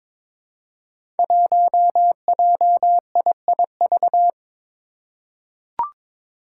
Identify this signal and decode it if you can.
Morse code, heard as a keyed tone, about 23 wpm, 700 Hz. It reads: 1JIIV
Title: Five Challenging Letter-Number Combinations 22wpm